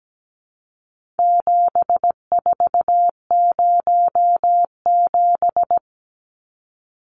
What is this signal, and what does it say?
Morse code, heard as a keyed tone, about 17 wpm, 700 Hz. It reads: 7407
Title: Numbers - 4-Digit 17wpm